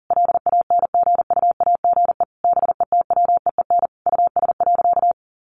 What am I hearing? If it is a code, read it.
Morse code, heard as a keyed tone, about 40 wpm, 700 Hz. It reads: LANGUAGE BETWEEN US.
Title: Sentences from Top 500 Words 40wpm